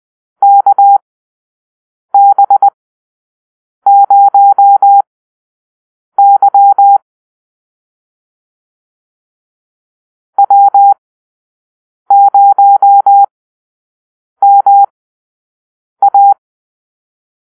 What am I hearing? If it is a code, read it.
Morse code, heard as a keyed tone, about 20 wpm, spaced as for 6 wpm, 800 Hz. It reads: KB0Y W0MA
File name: arrl / Call-Sign Drill-6